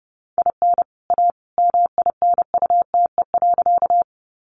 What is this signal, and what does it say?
Morse code, heard as a keyed tone, about 30 wpm, 700 Hz. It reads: IN A MINUTE.
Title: Sentences from Top 500 Words 30wpm